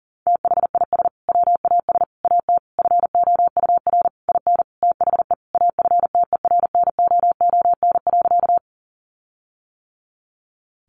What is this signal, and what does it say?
Morse code, heard as a keyed tone, about 40 wpm, 700 Hz. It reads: THIS WAS AT FOUR IN THE AFTERNOON.
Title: Sentences from Top 1500 Words 40wpm